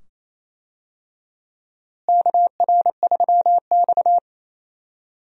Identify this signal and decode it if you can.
Morse code, heard as a keyed tone, about 28 wpm, 700 Hz. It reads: KR3X
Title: USA Call Signs - 4 Characters - Repeated 2x - 28wpm